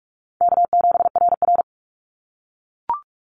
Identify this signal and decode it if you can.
Morse code, heard as a keyed tone, about 44 wpm, 700 Hz. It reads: K7RR
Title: Call Signs - 45wpm